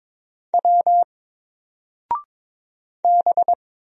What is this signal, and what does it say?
Morse code, heard as a keyed tone, about 22 wpm, 700 Hz. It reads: W B
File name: Single-Letter-Number-22wpm